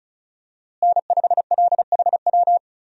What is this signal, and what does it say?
Morse code, heard as a keyed tone, about 36 wpm, 700 Hz. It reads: N5LHW
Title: USA Call Signs - 5 Characters - Repeated 3x - 35wpm